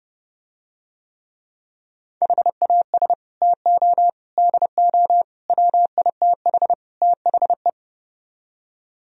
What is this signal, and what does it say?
Morse code, heard as a keyed tone, about 30 wpm, 700 Hz. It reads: HAS TO DO WITH THE